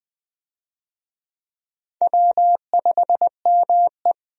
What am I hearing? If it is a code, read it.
Morse code, heard as a keyed tone, about 21 wpm, 700 Hz. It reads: W5ME
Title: USA Call Signs - 4 Characters - 20wpm